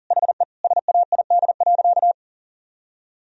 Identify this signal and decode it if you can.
Morse code, heard as a keyed tone, about 40 wpm, 700 Hz. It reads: HE SAID.
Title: Sentences from Top 1000 Words - Review Part 4 - 40wpm